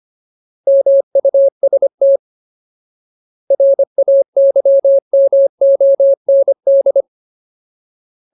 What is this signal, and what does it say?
Morse code, heard as a keyed tone, about 25 wpm, 550 Hz. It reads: MUST RAYMOND
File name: Ultimate Word List / 25WPM Ultimate